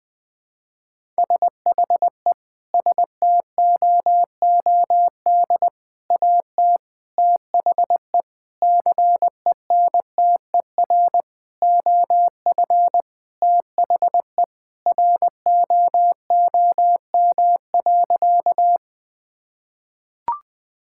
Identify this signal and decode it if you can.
Morse code, heard as a keyed tone, about 20 wpm, 700 Hz. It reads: SHE STOOD AT THE CENTER OF THE ROOM.
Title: Sentences with 5-Word Phrases - Spoken Beginning - 20wpm